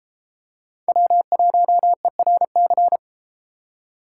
Morse code, heard as a keyed tone, about 32 wpm, 700 Hz. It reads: W1ERC